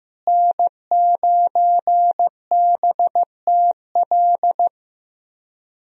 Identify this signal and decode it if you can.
Morse code, heard as a keyed tone, about 15 wpm, 700 Hz. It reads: N9BTL
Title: USA Call Signs - 5 Characters - Repeated 3x - 15wpm